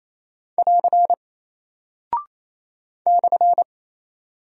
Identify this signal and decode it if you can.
Morse code, heard as a keyed tone, about 28 wpm, 700 Hz. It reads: <AR> /
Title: Single Character (Letter:Number:Punctuation:Prosign) Alt 28wpm